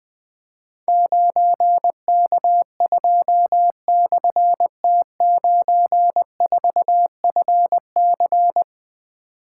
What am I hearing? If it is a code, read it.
Morse code, heard as a keyed tone, about 20 wpm, 700 Hz. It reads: 9K2/T94FC
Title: Difficult Call Signs 20wpm